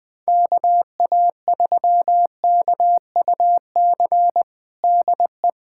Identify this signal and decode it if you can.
Morse code, heard as a keyed tone, about 20 wpm, 700 Hz. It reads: KA3KUC DE
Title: USA Callsign DE Callsign 20wpm